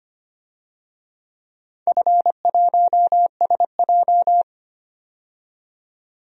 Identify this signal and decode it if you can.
Morse code, heard as a keyed tone, about 25 wpm, 700 Hz. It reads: F1SJ
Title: DXCC-100 Call Signs - 4 Characters - 25wpm